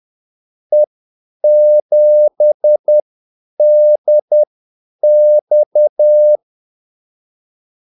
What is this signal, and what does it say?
Morse code, heard as a keyed tone, about 10 wpm, spaced as for 8 wpm, 600 Hz. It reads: E7DX
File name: Callsigns-10-8wpm